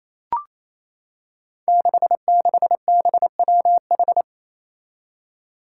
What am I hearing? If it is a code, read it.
Morse code, heard as a keyed tone, about 29 wpm, 700 Hz. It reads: 66BWH
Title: Five Challenging Letter-Number Combinations 28wpm